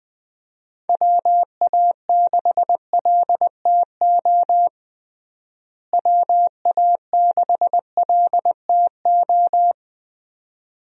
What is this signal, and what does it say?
Morse code, heard as a keyed tone, about 20 wpm, 700 Hz. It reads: WA6LTO WA6LTO
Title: USA Call Signs - 6 Characters - Repeated 2x - 20wpm